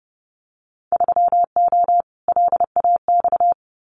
Morse code, heard as a keyed tone, about 30 wpm, 700 Hz. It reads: 3O LAX
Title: Winter Field Day Exchanges Encore 30wpm